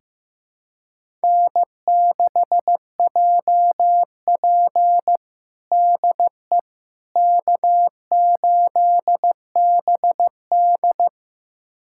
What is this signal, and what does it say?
Morse code, heard as a keyed tone, about 15 wpm, 700 Hz. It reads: N6JP DE K8BD
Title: USA Callsign DE Callsign - 4 Characters - 15wpm